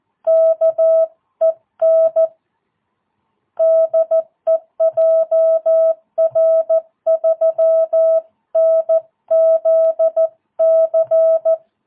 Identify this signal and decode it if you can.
Morse code, heard as a keyed tone, about 14 wpm, 660 Hz. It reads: KEN DEJR3NZC
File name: jh8cbh / 2022cw